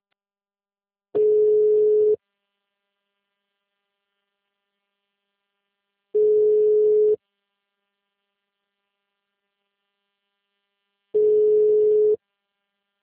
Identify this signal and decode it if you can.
Ringback tone (Europe, 425 Hz).